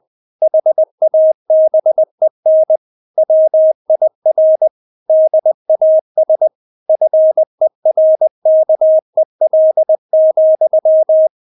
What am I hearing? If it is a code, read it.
Morse code, heard as a keyed tone, about 20 wpm, 630 Hz. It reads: HABEN WIR DAS FERKEL,
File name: Witz1 20wpm